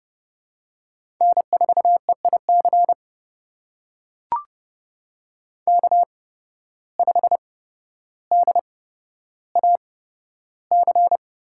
Morse code, heard as a keyed tone, about 29 wpm, spaced as for 13 wpm, 700 Hz. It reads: N4EIC K5DAC
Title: USA Call Signs - 5 Characters - 8x Character Spacing - 30wpm